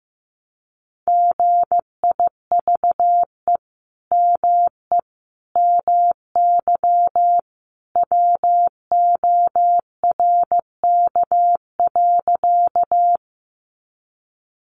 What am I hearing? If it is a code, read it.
Morse code, heard as a keyed tone, about 15 wpm, 700 Hz. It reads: GIVE ME MY WORK.
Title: Sentences from Top 200 Words 15wpm